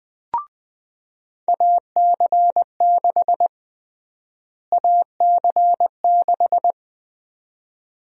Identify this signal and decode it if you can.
Morse code, heard as a keyed tone, about 20 wpm, 700 Hz. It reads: AC6 AC6
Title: USA Call Sign Prefixes - Repeated 2x - 20wpm